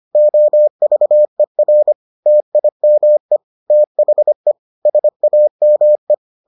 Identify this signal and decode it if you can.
Morse code, heard as a keyed tone, about 25 wpm, 600 Hz. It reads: OVER TIME THE SAME